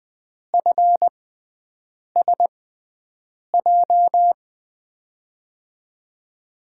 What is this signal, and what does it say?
Morse code, heard as a keyed tone, about 20 wpm, spaced as for 7 wpm, 700 Hz. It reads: FSJ
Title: USA Call Signs - 6 Characters - 6x Character Spacing - 20wpm